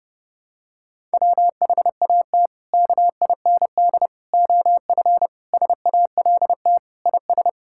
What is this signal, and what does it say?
Morse code, heard as a keyed tone, about 30 wpm, 700 Hz. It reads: WHAT KIND OF SALT IS
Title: Sentences from Top 1000 Words - Review Part 5 - 30wpm